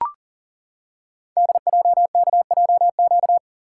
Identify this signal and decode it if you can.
Morse code, heard as a keyed tone, about 42 wpm, 690 Hz. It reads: DJKJQ
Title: Code Groups - 5 Letters - 40wpm